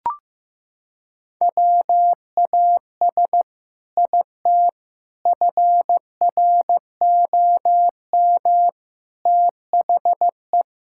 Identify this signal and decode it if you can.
Morse code, heard as a keyed tone, about 16 wpm, 700 Hz. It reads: WAS IT FROM THE
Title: Sentences from Top 1600 Words 15wpm